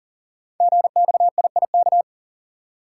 Morse code, heard as a keyed tone, about 42 wpm, 690 Hz. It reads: GXIIK
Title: Code Groups - 5 Letters - 40wpm